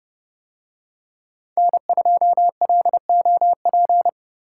Morse code, heard as a keyed tone, about 29 wpm, 700 Hz. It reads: N2LOP